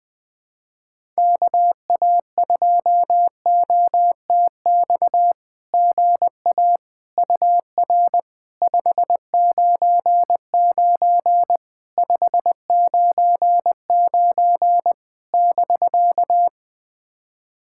Morse code, heard as a keyed tone, about 21 wpm, 690 Hz. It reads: KA2OTX GA UR 599 599 <BK>
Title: SOTA QSO Practice 20wpm